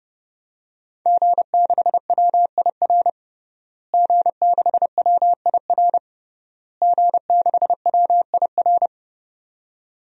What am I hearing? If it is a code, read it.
Morse code, heard as a keyed tone, about 30 wpm, 700 Hz. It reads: G6WIR G6WIR G6WIR
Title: DXCC-100 Call Signs - 5 Characters - Repeated 3x - 30wpm